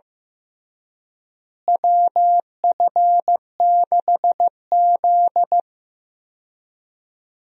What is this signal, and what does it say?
Morse code, heard as a keyed tone, about 15 wpm, 700 Hz. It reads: WF6Z
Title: USA Call Signs - 4 Characters - Repeated 3x - 15wpm